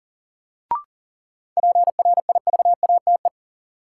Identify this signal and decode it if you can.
Morse code, heard as a keyed tone, about 40 wpm, 700 Hz. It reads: PRIVATE
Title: Sentences from Top 1000 Words - New Word Drill - 40wpm